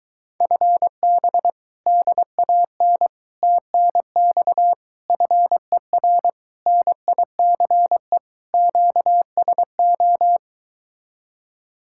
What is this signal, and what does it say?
Morse code, heard as a keyed tone, about 23 wpm, 700 Hz. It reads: FB DAN TNX FER NICE QSO